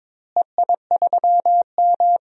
Morse code, heard as a keyed tone, about 22 wpm, 700 Hz. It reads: EI3M
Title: DXCC-100 Call Signs - 4 Characters - Repeated 3x - 22wpm